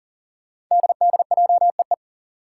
Morse code, heard as a keyed tone, about 40 wpm, 700 Hz. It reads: DDJEE
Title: Five Challenging Letter-Number Combinations 40wpm